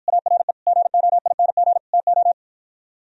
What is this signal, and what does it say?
Morse code, heard as a keyed tone, about 52 wpm, 700 Hz. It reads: ARE GOING TO